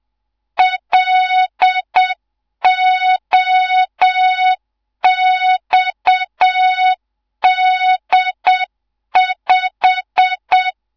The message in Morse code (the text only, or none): LOXD5